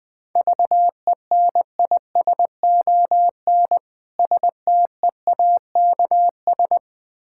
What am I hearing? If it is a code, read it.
Morse code, heard as a keyed tone, about 20 wpm, 700 Hz. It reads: VENISON STEAKS
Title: Difficult Words with Hints - 20wpm